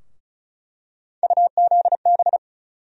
Morse code, heard as a keyed tone, about 36 wpm, 700 Hz. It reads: UZB
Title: USA Call Sign Postfixes - 3 Letters 35wpm